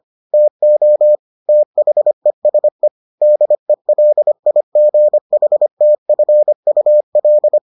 Morse code, heard as a keyed tone, about 25 wpm, 600 Hz. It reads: TO THESE DELIGHTFUL